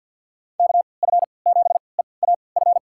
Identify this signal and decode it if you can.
Morse code, heard as a keyed tone, about 48 wpm, spaced as for 28 wpm, 700 Hz. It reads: KF7EAF
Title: USA Call Signs - 6 Characters - 3x Character Spacing - 50wpm